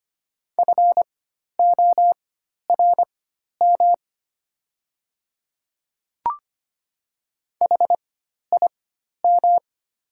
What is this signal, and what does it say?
Morse code, heard as a keyed tone, about 25 wpm, spaced as for 12 wpm, 700 Hz. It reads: FORM HIM
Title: 4x Character Spacing - Top 200 Words - 25wpm